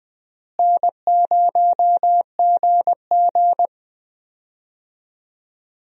Morse code, heard as a keyed tone, about 20 wpm, 700 Hz. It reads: N0GG
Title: USA Call Signs - 4 Characters - Repeated 3x - 20wpm